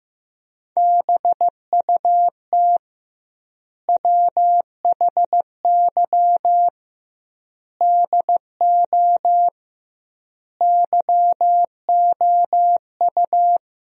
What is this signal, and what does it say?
Morse code, heard as a keyed tone, about 15 wpm, 700 Hz. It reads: BUT WHY DO YOU
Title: 2x Word-Spacing - Sentences from Top 200 Words - 15wpm